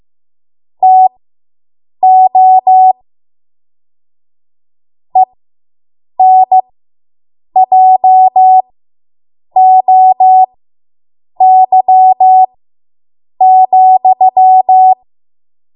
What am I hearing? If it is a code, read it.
Morse code, heard as a keyed tone, about 15 wpm, spaced as for 7 wpm, 750 Hz. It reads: TO ENJOY,